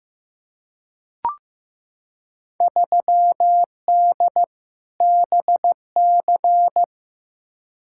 Morse code, heard as a keyed tone, about 15 wpm, 700 Hz. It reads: 3D BC
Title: Field Day Exchanges Encore - 15wpm